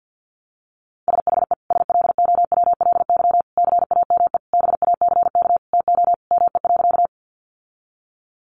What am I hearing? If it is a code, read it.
Morse code, heard as a keyed tone, about 50 wpm, 700 Hz. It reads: SHE SLOWLY CAME BACK TO ME.
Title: Sentences from Top 1100 Words 50wpm